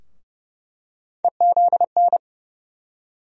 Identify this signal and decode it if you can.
Morse code, heard as a keyed tone, about 30 wpm, 700 Hz. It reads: EZN
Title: USA Call Sign Postfixes - 3 Letters 30wpm